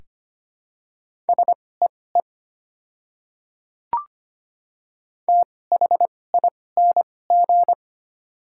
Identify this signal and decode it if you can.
Morse code, heard as a keyed tone, about 25 wpm, spaced as for 18 wpm, 700 Hz. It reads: SEE THING
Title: 2x Character Spacing - Top 100 Words - 25wpm